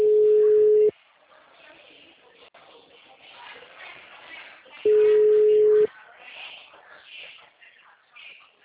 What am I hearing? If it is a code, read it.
Ringback tone (Europe, 425 Hz), with speech.